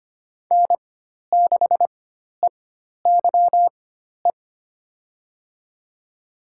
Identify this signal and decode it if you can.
Morse code, heard as a keyed tone, about 25 wpm, spaced as for 12 wpm, 700 Hz. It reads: N6EYE
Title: USA Call Signs - 4x Character Spacing - 25wpm